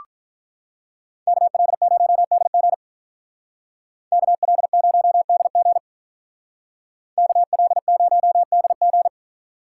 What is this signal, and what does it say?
Morse code, heard as a keyed tone, about 43 wpm, 700 Hz. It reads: KL0DG KL0DG KL0DG
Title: Speed Racing - USA 2x2 Call Signs 35wpm